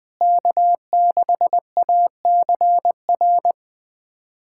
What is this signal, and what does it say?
Morse code, heard as a keyed tone, about 20 wpm, 700 Hz. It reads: K6ACR
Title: USA Callsign DE Callsign - 5 Characters - 20wpm